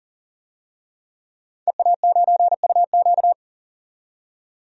Morse code, heard as a keyed tone, about 40 wpm, 700 Hz. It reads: EA9UQ
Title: DXCC-100 Call Signs - 5 Characters - 40wpm